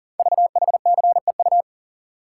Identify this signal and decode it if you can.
Morse code, heard as a keyed tone, about 38 wpm, 700 Hz. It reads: VHCEU
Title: Code Groups - 5 Letters - 40wpm